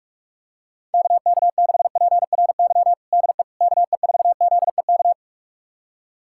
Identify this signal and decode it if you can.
Morse code, heard as a keyed tone, about 41 wpm, 690 Hz. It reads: KK6PRY DE KE4ZEK
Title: USA Callsign DE Callsign - 6 Characters - 45wpm